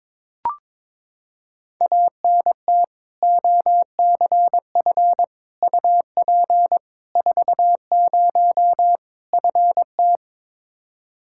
Morse code, heard as a keyed tone, about 22 wpm, 700 Hz. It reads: ANT OCF UP 40 FT
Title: K1USN SST QSO Practice 22wpm